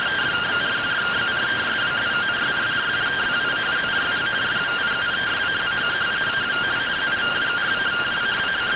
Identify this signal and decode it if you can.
Two-tone teleprinter signal, about 170 Hz shift.